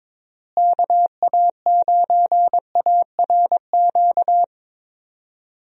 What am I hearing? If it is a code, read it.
Morse code, heard as a keyed tone, about 22 wpm, 700 Hz. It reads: KA9ARQ
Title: USA Call Signs - 6 Characters - Repeated 2x - 22wpm